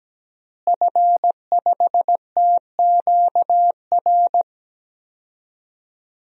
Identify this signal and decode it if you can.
Morse code, heard as a keyed tone, about 17 wpm, 700 Hz. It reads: F5TQR
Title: DXCC-100 Call Signs - 5 Characters - Repeated 3x - 17wpm